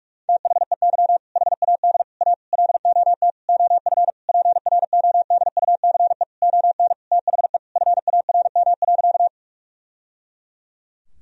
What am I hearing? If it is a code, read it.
Morse code, heard as a keyed tone, about 46 wpm, 700 Hz. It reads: THEY HAD A LOT OF PRODUCE ON THE FARM.
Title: Sentences from Top 1000 Words - Review Part 3 - 45wpm